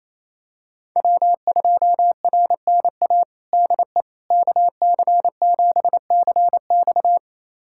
Morse code, heard as a keyed tone, about 28 wpm, 700 Hz. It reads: W2RNA DE KC7CX
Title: USA Callsign DE Callsign - 5 Characters - 28wpm